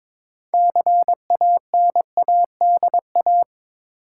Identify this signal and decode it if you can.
Morse code, heard as a keyed tone, about 22 wpm, 700 Hz. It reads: CANADA
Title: Sets of 5 Words - New Word Drill - 22wpm